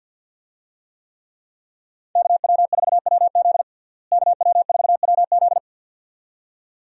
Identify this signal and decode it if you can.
Morse code, heard as a keyed tone, about 51 wpm, 700 Hz. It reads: KW4WZ KW4WZ
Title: USA Call Signs - 5 Characters - Repeated 2x - 50wpm